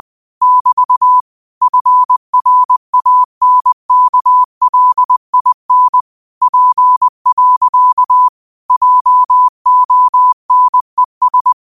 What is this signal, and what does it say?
Morse code, heard as a keyed tone, about 20 wpm, 1000 Hz. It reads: <BT> FRANKLIN P. JONES